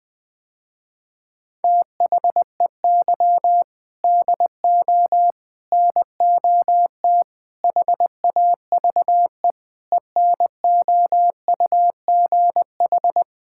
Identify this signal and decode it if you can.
Morse code, heard as a keyed tone, about 19 wpm, 700 Hz. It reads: THEY DO NOT HAVE ENOUGH
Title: Sentences with 4-Word Phrases - Spoken Beginning - 20wpm